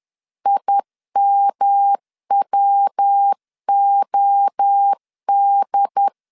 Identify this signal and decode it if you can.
Morse code, heard as a keyed tone, about 10 wpm, 800 Hz. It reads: IMWOD
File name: CW / DET / 1 3 1500C S C 10wpm 3335